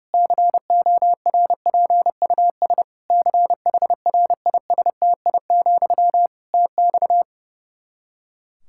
Morse code, heard as a keyed tone, about 30 wpm, 700 Hz. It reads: CORPUS CHRISTI, TX
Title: Practice QSO - Exchange 1 - Beginning - 30wpm